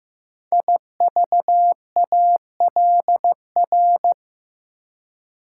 Morse code, heard as a keyed tone, about 15 wpm, 700 Hz. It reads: IVALR
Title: Code Groups - 5 Letters - 15wpm